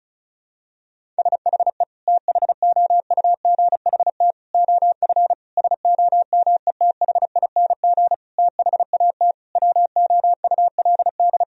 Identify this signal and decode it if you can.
Morse code, heard as a keyed tone, about 34 wpm, 700 Hz. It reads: SHE THOUGHT OF SOMETHING THAT WOULD
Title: Sentences from Top 1000 Words - Review Part 1 - 35wpm